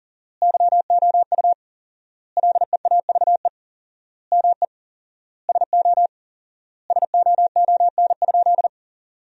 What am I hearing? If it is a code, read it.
Morse code, heard as a keyed tone, about 40 wpm, 700 Hz. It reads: YOU LEAVE ME SO SOON?
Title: 4x Word-Spacing - Sentences from Top 400 Words - 40wpm